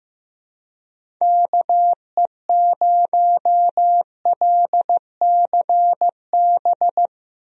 Morse code, heard as a keyed tone, about 15 wpm, 700 Hz. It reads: KE0LCB
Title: USA Call Signs - 6 Characters - Repeated 3x - 15wpm